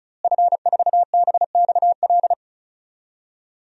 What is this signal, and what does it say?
Morse code, heard as a keyed tone, about 36 wpm, 700 Hz. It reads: F4BXL
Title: DXCC-100 Call Signs - 5 Characters - Repeated 3x - 35wpm